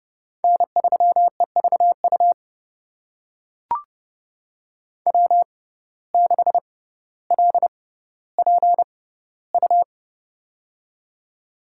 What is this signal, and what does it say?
Morse code, heard as a keyed tone, about 30 wpm, spaced as for 15 wpm, 700 Hz. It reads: N3EVU W6LPU
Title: USA Call Signs - 5 Characters - 6x Character Spacing - 30wpm